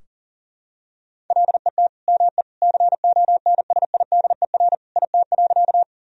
Morse code, heard as a keyed tone, about 40 wpm, 700 Hz. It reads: LET ME CONSIDER IT.
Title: Sentences from Top 1000 Words - Review Part 4 - 40wpm